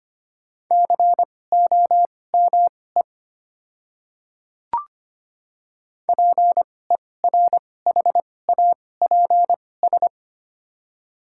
Morse code, heard as a keyed tone, about 25 wpm, spaced as for 18 wpm, 700 Hz. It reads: COME PERHAPS
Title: 2x Character Spacing - Top 500 Words - 25wpm